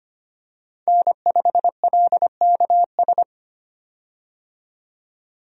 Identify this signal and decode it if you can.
Morse code, heard as a keyed tone, about 25 wpm, 700 Hz. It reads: N5LKS
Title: USA Call Signs - 5 Characters - 25wpm